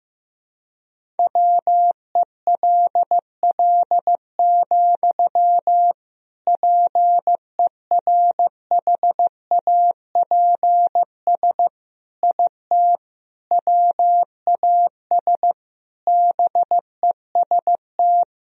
Morse code, heard as a keyed tone, about 15 wpm, 700 Hz. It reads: WELL, PERHAPS IT WAS BEST
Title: Sentences from Top 500 Words and Abbreviations 15wpm